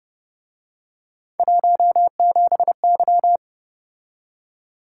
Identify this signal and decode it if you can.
Morse code, heard as a keyed tone, about 30 wpm, 700 Hz. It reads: 17Y